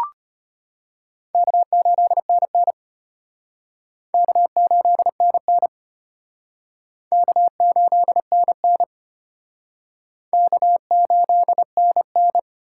Morse code, heard as a keyed tone, about 32 wpm, 700 Hz. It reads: K8NN K8NN K8NN K8NN